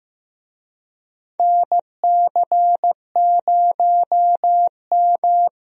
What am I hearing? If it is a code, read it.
Morse code, heard as a keyed tone, about 15 wpm, 700 Hz. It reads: NC0M
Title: USA Call Signs - 4 Characters - 15wpm